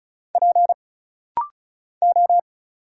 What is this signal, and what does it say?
Morse code, heard as a keyed tone, about 34 wpm, 700 Hz. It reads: P O
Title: Single Letter-Number Alt - Rapid-Fire 35wpm